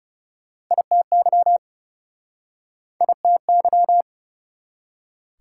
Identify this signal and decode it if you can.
Morse code, heard as a keyed tone, about 41 wpm, 690 Hz. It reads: ITY ITY